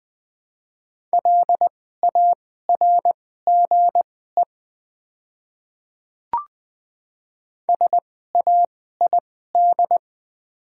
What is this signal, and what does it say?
Morse code, heard as a keyed tone, about 20 wpm, spaced as for 14 wpm, 700 Hz. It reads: LARGE SAID
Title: 2x Character Spacing - Top 200 Words - 20wpm